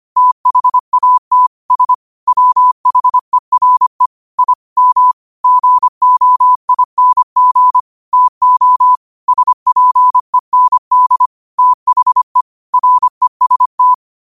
Morse code, heard as a keyed tone, about 25 wpm, 1000 Hz. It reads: THAT S WHERE I M GOING TO SPEND THE REST
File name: qotdfiles / 25wpm